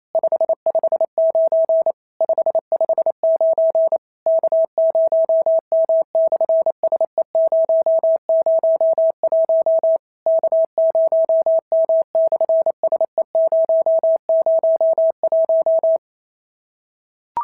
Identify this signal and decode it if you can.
Morse code, heard as a keyed tone, about 27 wpm, 650 Hz. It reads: 559 559 K0M/SE001 K0M/SE001
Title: SOTA QSO Practice 28wpm